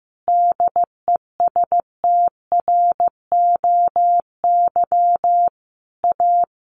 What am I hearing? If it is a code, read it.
Morse code, heard as a keyed tone, about 15 wpm, 700 Hz. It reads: DESTROY A
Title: A Sound Of Thunder 15wpm